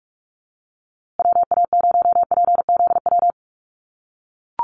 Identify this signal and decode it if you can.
Morse code, heard as a keyed tone, about 44 wpm, 700 Hz. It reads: WA0PZW
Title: USA Call Signs Repeated 3x 45wpm